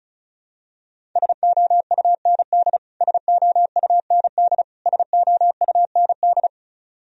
Morse code, heard as a keyed tone, about 34 wpm, 700 Hz. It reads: SOUND SOUND SOUND
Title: Top 100 Words Repeated 35wpm